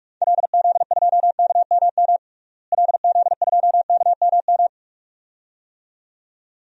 Morse code, heard as a keyed tone, about 46 wpm, 700 Hz. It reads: LZ2KMM LZ2KMM
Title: DXCC-100 Call Signs - 6 Characters - Repeated 2x - 45wpm